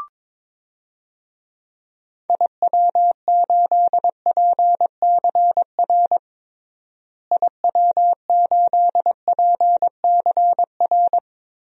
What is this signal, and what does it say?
Morse code, heard as a keyed tone, about 22 wpm, 700 Hz. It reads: IW8PCR IW8PCR
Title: DXCC-100 Call Signs - 6 Characters - Repeated 3x - 22wpm